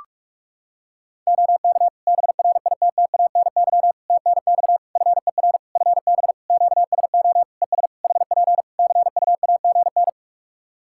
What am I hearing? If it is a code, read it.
Morse code, heard as a keyed tone, about 48 wpm, 700 Hz. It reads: OK BRITTANY TNX FER FB QSO ES HP CUAGN
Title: Practice QSO - Exchange 4 - Ending - 45wpm